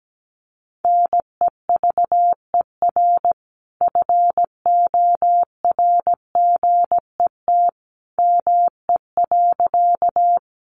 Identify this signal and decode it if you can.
Morse code, heard as a keyed tone, about 17 wpm, 700 Hz. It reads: NEVER FORGET ME.
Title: Sentences from Top 1200 Words 17wpm